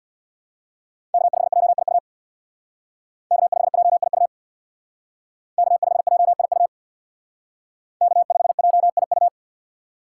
Morse code, heard as a keyed tone, about 64 wpm, 700 Hz. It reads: K5JIU K5JIU K5JIU K5JIU